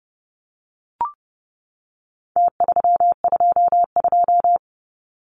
Morse code, heard as a keyed tone, about 30 wpm, 700 Hz. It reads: T322
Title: Numbers - Contest Style - 4-Digit 30wpm